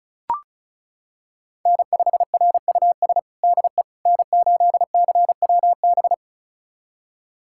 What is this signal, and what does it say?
Morse code, heard as a keyed tone, about 36 wpm, 690 Hz. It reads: N5RUS DE N8CWB
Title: USA Callsign DE Callsign - 5 Characters - 35wpm